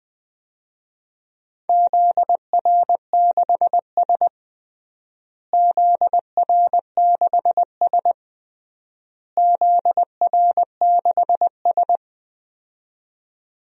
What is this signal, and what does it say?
Morse code, heard as a keyed tone, about 21 wpm, 700 Hz. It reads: ZR6S ZR6S ZR6S
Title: DXCC-100 Call Signs - 4 Characters - Repeated 3x - 20wpm